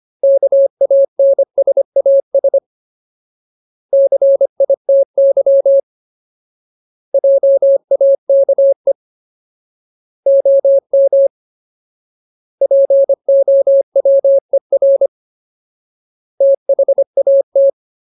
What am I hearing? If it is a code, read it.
Morse code, heard as a keyed tone, about 25 wpm, 550 Hz. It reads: KANSAS CITY JAKE OM POWER THAT
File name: Ultimate Word List / 25WPM Ultimate